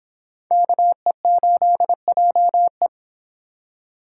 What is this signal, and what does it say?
Morse code, heard as a keyed tone, about 26 wpm, 700 Hz. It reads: KE8JE